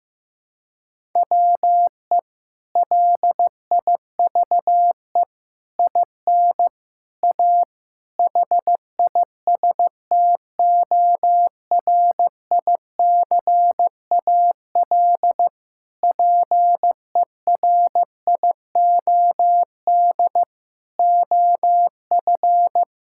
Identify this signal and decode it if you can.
Morse code, heard as a keyed tone, about 15 wpm, 700 Hz. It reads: WE LIVE IN A HISTORICAL PERIOD OF
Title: Sentences from Top 1000 Words - Review Part 5 - 15wpm